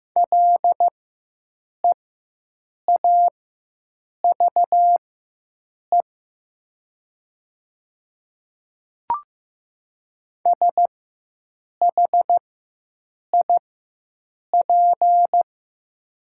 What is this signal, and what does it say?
Morse code, heard as a keyed tone, about 15 wpm, spaced as for 7 wpm, 700 Hz. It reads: LEAVE SHIP
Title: 4x Character Spacing - Top 500 Words - 15wpm